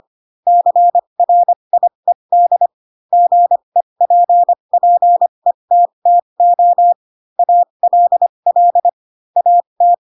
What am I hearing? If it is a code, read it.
Morse code, heard as a keyed tone, about 25 wpm, spaced as for 22 wpm, 700 Hz. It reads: CRIED GEPPETTO ALL AT